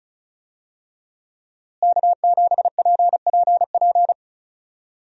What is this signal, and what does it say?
Morse code, heard as a keyed tone, about 34 wpm, 700 Hz. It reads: K7PPP
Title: USA Call Signs - 5 Characters - Repeated 2x - 35wpm